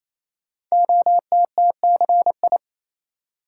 Morse code, heard as a keyed tone, about 28 wpm, 690 Hz. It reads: OTTCI
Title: Code Groups - 5 Letters - 28wpm